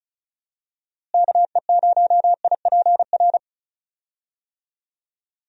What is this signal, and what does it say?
Morse code, heard as a keyed tone, about 34 wpm, 700 Hz. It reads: KE0IPR
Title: USA Call Signs - 6 Characters - Repeated 3x - 35wpm